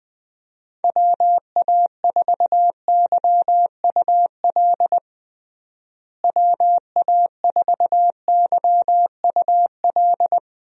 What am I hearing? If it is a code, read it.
Morse code, heard as a keyed tone, about 20 wpm, 700 Hz. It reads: WA4YUL WA4YUL
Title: USA Call Signs - 6 Characters - Repeated 3x - 20wpm